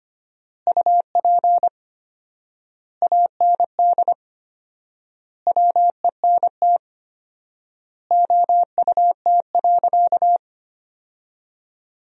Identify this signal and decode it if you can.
Morse code, heard as a keyed tone, about 25 wpm, 700 Hz. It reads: UP AND WENT OUT.
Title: 4x Word-Spacing - Sentences from Top 300 Words - 25wpm